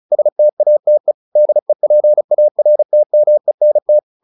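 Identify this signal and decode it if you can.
Morse code, heard as a keyed tone, about 34 wpm, 600 Hz. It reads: STATE DEPARTMENT
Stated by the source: Morse practice at 35 wpm from CW med Gnister, sending Ken Follett's novel Never in English.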